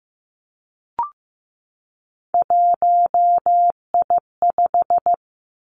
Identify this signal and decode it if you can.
Morse code, heard as a keyed tone, about 15 wpm, 700 Hz. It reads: 1I5
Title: Three Challenging Letter-Number Combinations 15wpm